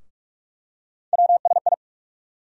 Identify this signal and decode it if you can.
Morse code, heard as a keyed tone, about 45 wpm, 700 Hz. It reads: WSI
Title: USA Call Sign Postfixes - 3 Letters 45wpm